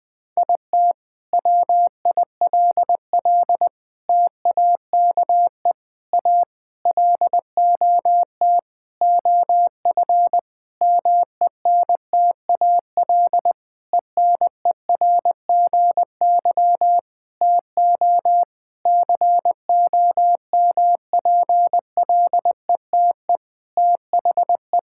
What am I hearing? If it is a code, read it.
Morse code, heard as a keyed tone, about 20 wpm, 700 Hz. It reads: IT WILL TAKE A LOT OF MENTAL ENERGY TO COMPLETE THE
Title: Sentences from Top 1000 Words - Review Part 6 - 20wpm